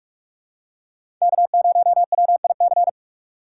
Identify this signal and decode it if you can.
Morse code, heard as a keyed tone, about 46 wpm, 700 Hz. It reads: K0WIC
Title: USA Call Signs - 5 Characters - 2x Character Spacing - 45wpm